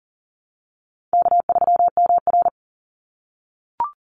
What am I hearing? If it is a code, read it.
Morse code, heard as a keyed tone, about 40 wpm, 700 Hz. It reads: K3MR
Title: Call Signs - 40wpm